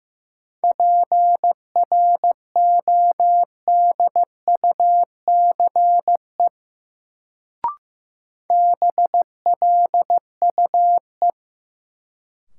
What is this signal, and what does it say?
Morse code, heard as a keyed tone, about 15 wpm, 700 Hz. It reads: PRODUCE BLUE
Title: Sentences from Top 1000 Words - Review Part 3 - New Word Drill - 15wpm